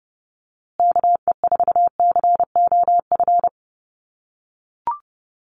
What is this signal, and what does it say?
Morse code, heard as a keyed tone, about 30 wpm, 700 Hz. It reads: KE4COF
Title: USA Call Signs Repeated 3x 30wpm